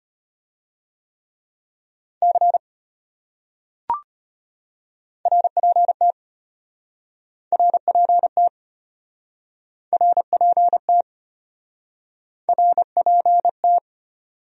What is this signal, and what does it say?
Morse code, heard as a keyed tone, about 33 wpm, 700 Hz. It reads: C RPT RPT RPT RPT